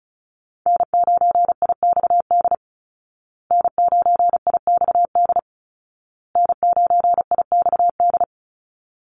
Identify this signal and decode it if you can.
Morse code, heard as a keyed tone, about 34 wpm, 700 Hz. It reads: N9IXD N9IXD N9IXD
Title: Call Signs Repeated 2x - 35wpm